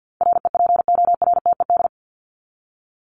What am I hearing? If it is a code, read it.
Morse code, heard as a keyed tone, about 50 wpm, 700 Hz. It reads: REPORTED
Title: ED Suffix - Top 100 Words with Base Words 50wpm